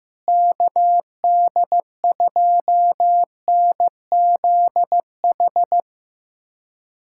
Morse code, heard as a keyed tone, about 15 wpm, 700 Hz. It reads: KD2NZH
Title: USA Callsign DE Callsign 15wpm